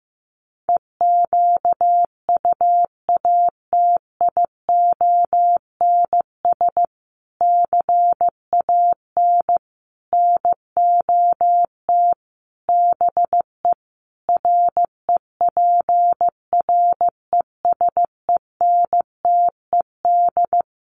Morse code, heard as a keyed tone, about 15 wpm, 700 Hz. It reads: EQUATIONS CAN NOT BE REPRESENTED
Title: Sets of 5 Words 15wpm